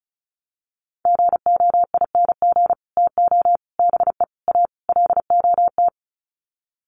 Morse code, heard as a keyed tone, about 34 wpm, 700 Hz. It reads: GOING TO BE A LOT
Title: Sets of 5 Words Encore 35wpm